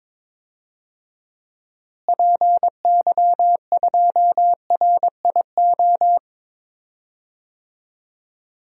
Morse code, heard as a keyed tone, about 22 wpm, 700 Hz. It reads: PY2RIO
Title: DXCC-100 Call Signs - 6 Characters - Repeated 3x - 22wpm